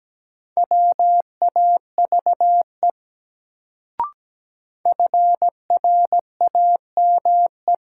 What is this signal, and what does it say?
Morse code, heard as a keyed tone, about 17 wpm, 700 Hz. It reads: WAVE FRAME
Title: Sentences from Top 1400 Words - New Word Drill Alt - 17wpm